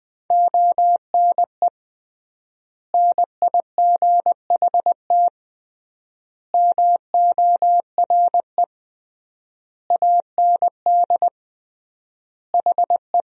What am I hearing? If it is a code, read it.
Morse code, heard as a keyed tone, about 20 wpm, 700 Hz. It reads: ONE NIGHT MORE AND HE
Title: 3x Word-Spacing - Sentences from Top 300 Words - 20wpm